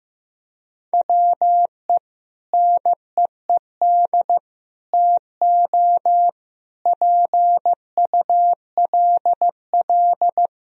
Morse code, heard as a keyed tone, about 15 wpm, 700 Hz. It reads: WE NEED TO PULL
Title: Sentences from Top 1000 Words - Review Part 2 - 15wpm